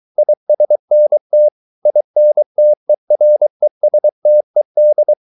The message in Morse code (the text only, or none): ISNT INTERESTED